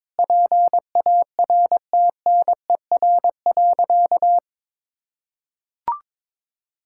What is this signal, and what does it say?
Morse code, heard as a keyed tone, about 21 wpm, 700 Hz. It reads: PARTNER.
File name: practice / 22wpm / Sentences with 8-Word Phrases - 22wpm